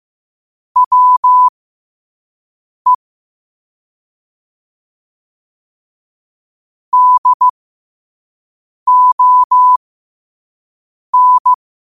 Morse code, heard as a keyed tone, about 15 wpm, spaced as for 5 wpm, 1000 Hz. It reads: WE DON